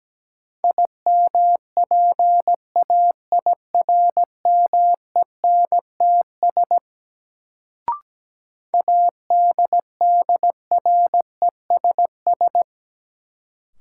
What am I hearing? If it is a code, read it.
Morse code, heard as a keyed tone, about 17 wpm, 700 Hz. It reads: IMPAIRMENTS ADDRESS
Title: Sets of 5 Words Encore - New Word Drill - 17wpm